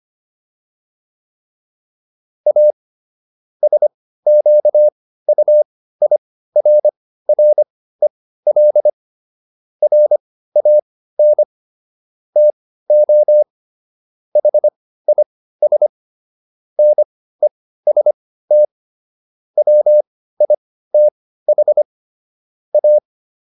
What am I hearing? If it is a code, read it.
Morse code, heard as a keyed tone, about 25 wpm, spaced as for 15 wpm, 600 Hz. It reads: A SQUIRREL RAN TO HIS NEST WITH A